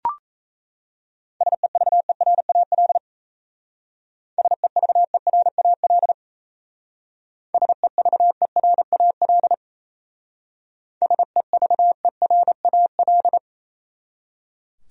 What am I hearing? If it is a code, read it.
Morse code, heard as a keyed tone, about 36 wpm, 700 Hz. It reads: SEVERAL SEVERAL SEVERAL SEVERAL